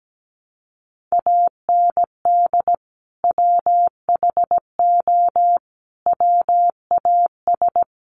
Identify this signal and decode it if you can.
Morse code, heard as a keyed tone, about 17 wpm, 700 Hz. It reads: AND WHO WAS
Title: Sentences from Top 100 Words 17wpm